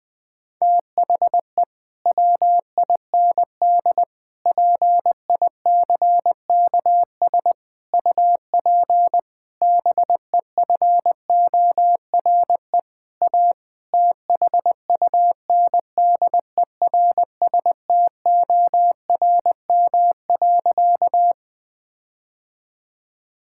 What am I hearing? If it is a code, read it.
Morse code, heard as a keyed tone, about 21 wpm, 700 Hz. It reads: THE WIND PICKS UP BEFORE A THUNDERSTORM.
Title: Difficult Sentences with Hints - 20wpm